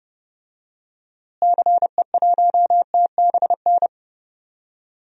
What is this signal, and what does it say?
Morse code, heard as a keyed tone, about 29 wpm, 700 Hz. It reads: CE1TBN
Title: DXCC-100 Call Signs - 6 Characters - Repeated 3x - 30wpm